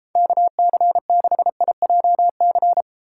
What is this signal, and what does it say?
Morse code, heard as a keyed tone, about 32 wpm, 700 Hz. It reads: KC6IJC